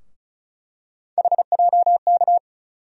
Morse code, heard as a keyed tone, about 36 wpm, 700 Hz. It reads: HJK
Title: USA Call Sign Postfixes - 3 Letters 35wpm